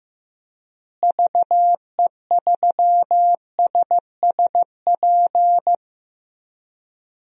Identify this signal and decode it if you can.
Morse code, heard as a keyed tone, about 15 wpm, 700 Hz. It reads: VE3SSP
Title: DXCC-100 Call Signs - 6 Characters - Repeated 3x - 15wpm